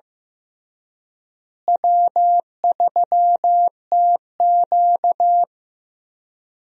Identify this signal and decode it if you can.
Morse code, heard as a keyed tone, about 15 wpm, 700 Hz. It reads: W3TQ
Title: USA Call Signs - 4 Characters - Repeated 2x - 15wpm